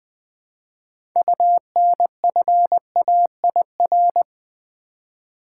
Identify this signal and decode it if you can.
Morse code, heard as a keyed tone, about 20 wpm, 700 Hz. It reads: UNFAIR
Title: 6 Letter Words - First 3 Letters Given - 20wpm